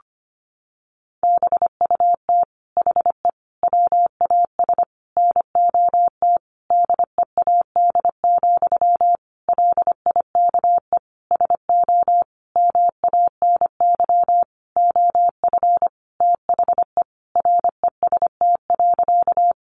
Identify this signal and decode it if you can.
Morse code, heard as a keyed tone, about 25 wpm, 700 Hz. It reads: BUT HE WAS NOT DEAD, LIKE SO MANY OF THE REST.
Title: Sentences from Top 1500 Words 25wpm